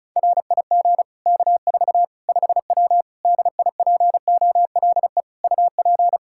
Morse code, heard as a keyed tone, about 34 wpm, 700 Hz. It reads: RIG K4 5W DIPOLE UP
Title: Practice QSO - Exchange 2 - Info - 35wpm